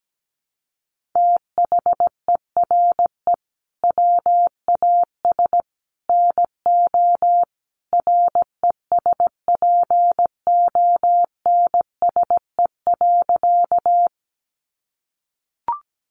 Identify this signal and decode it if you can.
Morse code, heard as a keyed tone, about 17 wpm, 700 Hz. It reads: THERE WAS NO RESPONSE.
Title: Sentences from Top 1400 Words 17wpm